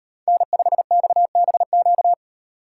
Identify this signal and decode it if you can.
Morse code, heard as a keyed tone, about 36 wpm, 700 Hz. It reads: N5XBQ